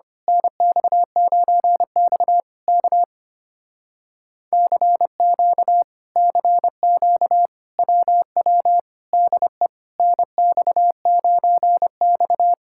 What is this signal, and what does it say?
Morse code, heard as a keyed tone, about 24 wpm, 700 Hz. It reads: NX9X K CQ CQ WW DE NX9X
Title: Speed-Racing Common QSO Phrases Encore - 25wpm